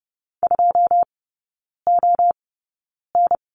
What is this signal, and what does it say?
Morse code, heard as a keyed tone, about 30 wpm, spaced as for 9 wpm, 700 Hz. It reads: 2ON
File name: 081 Lesson 14e - Number 2 - Three Characters 30wpm